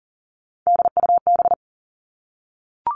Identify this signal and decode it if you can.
Morse code, heard as a keyed tone, about 40 wpm, 700 Hz. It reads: DUB
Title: Three Challenging Letter-Number Combinations 40wpm